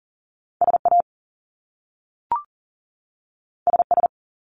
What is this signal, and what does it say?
Morse code, heard as a keyed tone, about 40 wpm, 700 Hz. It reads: SA SS
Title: Top Two Letter Combinations in Words 40wpm